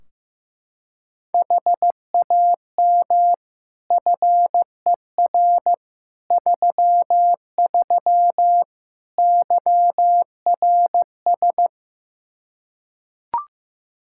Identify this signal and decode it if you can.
Morse code, heard as a keyed tone, about 15 wpm, 700 Hz. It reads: HAM FER 33 YRS
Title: Practice QSO - Exchange 3 - More Info - 15wpm